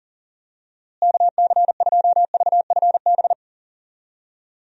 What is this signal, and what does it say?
Morse code, heard as a keyed tone, about 37 wpm, 700 Hz. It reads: KC2VFB
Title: USA Call Signs - 6 Characters - Repeated 2x - 40wpm